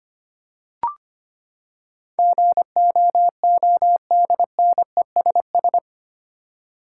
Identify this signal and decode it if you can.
Morse code, heard as a keyed tone, about 25 wpm, 700 Hz. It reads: GOODNESS
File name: Eight Letter Words - 25wpm